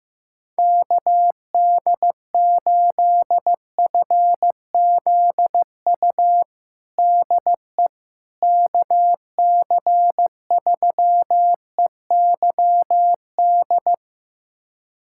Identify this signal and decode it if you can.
Morse code, heard as a keyed tone, about 15 wpm, 700 Hz. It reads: KD8FZU DE KC3EYD
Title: USA Callsign DE Callsign 15wpm